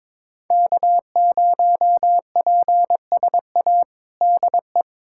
Morse code, heard as a keyed tone, about 22 wpm, 700 Hz. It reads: K0PSA DE
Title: USA Callsign DE Callsign - 5 Characters - 22wpm